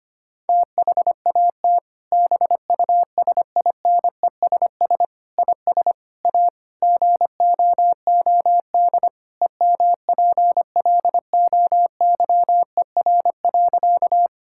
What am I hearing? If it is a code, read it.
Morse code, heard as a keyed tone, about 25 wpm, 700 Hz. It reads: THAT BUSINESS IS A GOOD EMPLOYER.
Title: Sentences from Top 1600 Words 25wpm